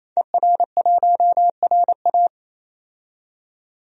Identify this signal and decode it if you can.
Morse code, heard as a keyed tone, about 28 wpm, 700 Hz. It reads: ER1RA
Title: DXCC-100 Call Signs - 5 Characters - 28wpm